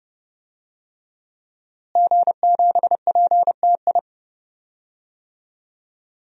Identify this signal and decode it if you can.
Morse code, heard as a keyed tone, about 30 wpm, 700 Hz. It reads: G7PTI
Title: DXCC-100 Call Signs - 5 Characters - Repeated 3x - 30wpm